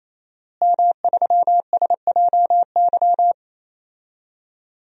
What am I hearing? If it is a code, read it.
Morse code, heard as a keyed tone, about 28 wpm, 700 Hz. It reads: M3SJY
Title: DXCC-100 Call Signs - 5 Characters - Repeated 3x - 28wpm